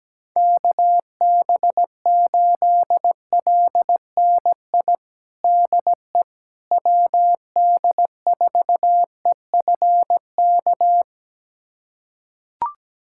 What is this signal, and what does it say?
Morse code, heard as a keyed tone, about 17 wpm, 700 Hz. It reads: KB8LNI DE WD4EFK
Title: USA Callsign DE Callsign 17wpm